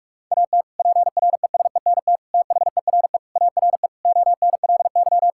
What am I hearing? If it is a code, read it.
Morse code, heard as a keyed tone, about 41 wpm, 700 Hz. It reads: AT PRESENT THERE ARE ONLY
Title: Sentences from Top 1000 Words - Review Part 5 - 45wpm